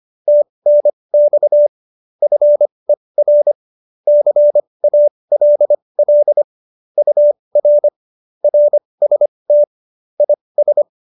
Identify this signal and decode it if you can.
Morse code, heard as a keyed tone, about 25 wpm, spaced as for 20 wpm, 590 Hz. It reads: TNX FER CALL UR RST IS